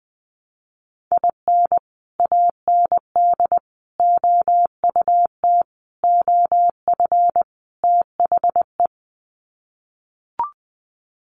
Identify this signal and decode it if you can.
Morse code, heard as a keyed tone, about 20 wpm, 700 Hz. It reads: IN AND OUT OF THE
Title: Sets of 5 Words 20wpm